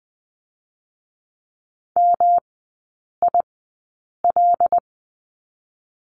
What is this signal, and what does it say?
Morse code, heard as a keyed tone, about 20 wpm, 700 Hz. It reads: M I L
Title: 184 Lesson 30e - Letter b - Three Characters 20wpm